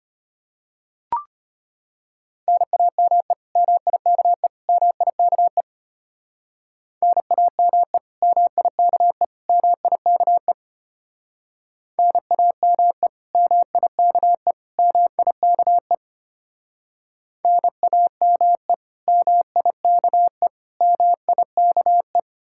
Morse code, heard as a keyed tone, about 32 wpm, 700 Hz. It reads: NAME MIKE MIKE NAME MIKE MIKE NAME MIKE MIKE NAME MIKE MIKE